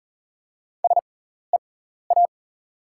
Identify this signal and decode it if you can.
Morse code, heard as a keyed tone, about 40 wpm, spaced as for 14 wpm, 690 Hz. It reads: SEA
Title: 6x Character Spacing - Top 500 Words - 40wpm